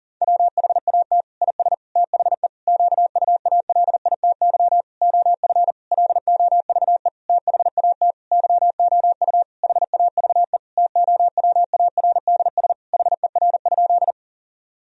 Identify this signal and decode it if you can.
Morse code, heard as a keyed tone, about 40 wpm, 700 Hz. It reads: WHAT IS THE QUALITY OF LOVE THAT YOU HAVE TOWARDS HER?
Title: Sentences from Top 1000 Words - Review Part 4 - 40wpm